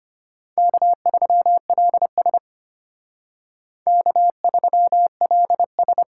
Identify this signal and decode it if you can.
Morse code, heard as a keyed tone, about 32 wpm, 700 Hz. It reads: K3LS K3LS